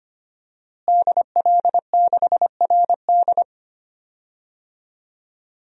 Morse code, heard as a keyed tone, about 25 wpm, 700 Hz. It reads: DL6RD
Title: DXCC-100 Call Signs - 5 Characters - Repeated 2x - 25wpm